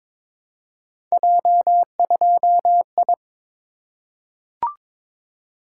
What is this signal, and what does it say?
Morse code, heard as a keyed tone, about 22 wpm, 700 Hz. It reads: J2I
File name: practice / 22wpm / Three Letter-Number Combinations - 22wpm